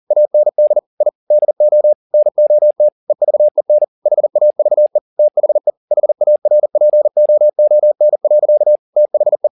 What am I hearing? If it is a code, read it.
Morse code, heard as a keyed tone, about 40 wpm, 600 Hz. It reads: AND I DO NOT EVEN HAVE THE HARPOON. THE
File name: the old man and the sea 2 40wpm